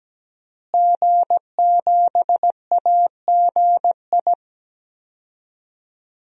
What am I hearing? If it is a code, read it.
Morse code, heard as a keyed tone, about 17 wpm, 700 Hz. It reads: G7AGI
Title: DXCC-100 Call Signs - 5 Characters - Repeated 3x - 17wpm